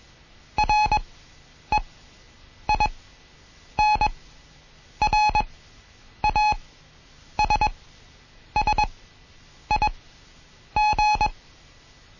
Morse code, heard as a keyed tone, about 21 wpm, spaced as for 8 wpm, 850 Hz. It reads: REINRASSIG
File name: Morsen / Kurs A 20